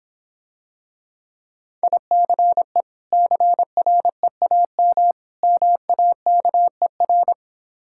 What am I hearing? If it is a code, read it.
Morse code, heard as a keyed tone, about 26 wpm, 700 Hz. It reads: ICE CREAM MAKER